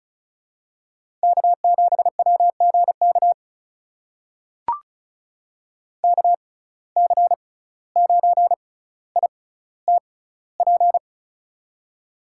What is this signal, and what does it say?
Morse code, heard as a keyed tone, about 34 wpm, spaced as for 17 wpm, 700 Hz. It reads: K7WGK KC9ITP
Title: USA Call Signs - 6x Character Spacing - 35wpm